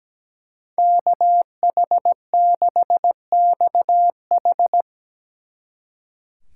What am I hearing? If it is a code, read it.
Morse code, heard as a keyed tone, about 19 wpm, 700 Hz. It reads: KH6XH
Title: Speed Racing - USA 2x2 Call Signs 17wpm